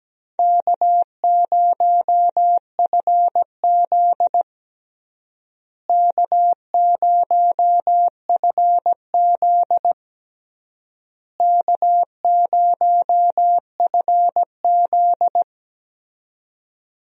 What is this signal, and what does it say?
Morse code, heard as a keyed tone, about 17 wpm, 700 Hz. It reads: K0FZ K0FZ K0FZ
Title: USA Call Signs - 4 Characters - Repeated 3x - 17wpm